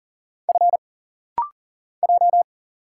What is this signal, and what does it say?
Morse code, heard as a keyed tone, about 40 wpm, 700 Hz. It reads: F J
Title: Single Letter-Number Alt - Rapid-Fire 40wpm